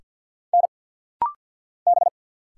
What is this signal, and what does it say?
Morse code, heard as a keyed tone, about 50 wpm, 700 Hz. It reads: N B
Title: Single Letters - Rapid-Fire 50wpm